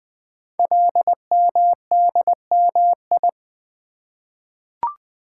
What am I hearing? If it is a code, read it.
Morse code, heard as a keyed tone, about 20 wpm, 700 Hz. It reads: LMDMI
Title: Code Groups - 5 Letters - 20wpm